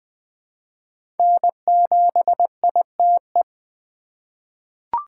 Morse code, heard as a keyed tone, about 20 wpm, 700 Hz. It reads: N7ITE
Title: USA Call Signs - 5 Characters - 3x Character Spacing - 20wpm